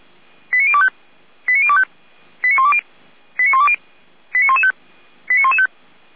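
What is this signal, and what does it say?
Five-tone selective call (ZVEI), heard as quick runs of five tones, tones of about 70 ms. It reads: 89025 89025 89120 89120 89115 89115
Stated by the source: ZVEI five-tone call received by SDR.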